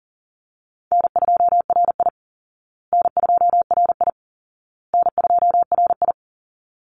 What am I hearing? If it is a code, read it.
Morse code, heard as a keyed tone, about 40 wpm, 700 Hz. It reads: N2RI N2RI N2RI
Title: Call Signs Repeated 2x - 40wpm